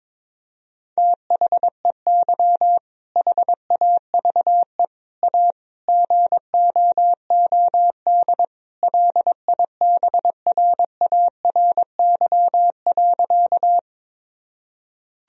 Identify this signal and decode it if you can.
Morse code, heard as a keyed tone, about 21 wpm, 700 Hz. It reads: THEY HAVE A GOOD LIBRARY.
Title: Sentences from Top 1000 Words - Review Part 4 - 22wpm